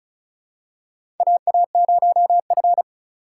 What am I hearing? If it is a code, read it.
Morse code, heard as a keyed tone, about 36 wpm, 700 Hz. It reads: AA0F
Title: USA Call Signs - 4 Characters - 2x Character Spacing - 35wpm